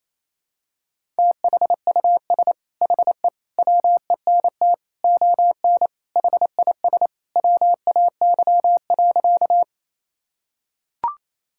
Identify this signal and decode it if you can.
Morse code, heard as a keyed tone, about 28 wpm, 700 Hz. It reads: THUS HE WENT ON HIS WAY.
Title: Sentences from Top 1000 Words - Review Part 5 - 28wpm